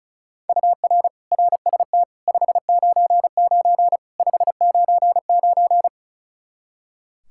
Morse code, heard as a keyed tone, about 36 wpm, 700 Hz. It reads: UR RST 599 599
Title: Practice QSO - Exchange 1 - Beginning - 35wpm